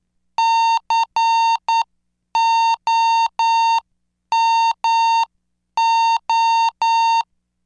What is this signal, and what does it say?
Morse code, heard as a keyed tone, about 9 wpm, 910 Hz. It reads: COMO